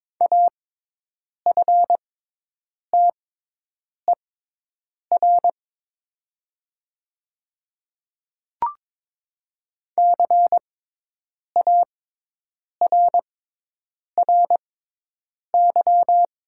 Morse code, heard as a keyed tone, about 22 wpm, spaced as for 8 wpm, 700 Hz. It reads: AFTER CARRY
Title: 6x Character Spacing - Top 300 Words - 22wpm